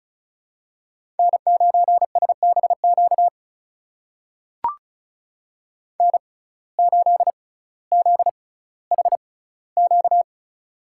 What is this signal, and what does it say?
Morse code, heard as a keyed tone, about 36 wpm, spaced as for 18 wpm, 700 Hz. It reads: N9SBQ N8ZHQ
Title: USA Call Signs - 5 Characters - 6x Character Spacing - 35wpm